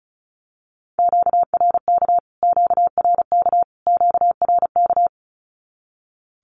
Morse code, heard as a keyed tone, about 34 wpm, 700 Hz. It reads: QRK QRK QRK
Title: Q-Codes Repeated 3x 35wpm